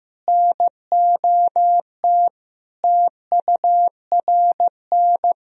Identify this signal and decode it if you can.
Morse code, heard as a keyed tone, about 15 wpm, 700 Hz. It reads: NOT TURN
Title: Sentences from Top 1000 Words - Review Part 2 - 15wpm